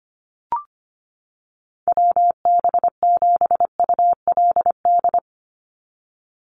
Morse code, heard as a keyed tone, about 25 wpm, 700 Hz. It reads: WB7ULD
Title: NA SOTA Chaser Callsigns 25wpm